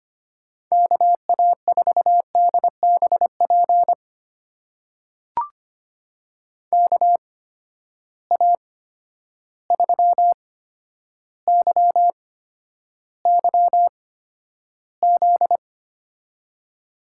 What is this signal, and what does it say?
Morse code, heard as a keyed tone, about 25 wpm, spaced as for 11 wpm, 700 Hz. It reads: KA4DBP KA3YYZ
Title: USA Call Signs - 6 Characters - 8x Character Spacing - 25wpm